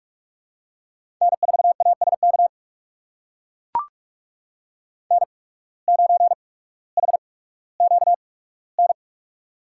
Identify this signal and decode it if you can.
Morse code, heard as a keyed tone, about 43 wpm, spaced as for 19 wpm, 700 Hz. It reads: N4ASK N9HQN
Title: USA Call Signs - 5 Characters - 8x Character Spacing - 45wpm